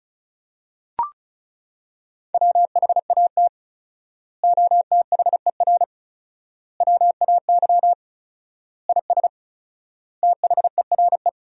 Morse code, heard as a keyed tone, about 36 wpm, 700 Hz. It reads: WHAT OTHER WAY IS THERE
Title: 4x Word-Spacing - Sentences from Top 100 Words - 35wpm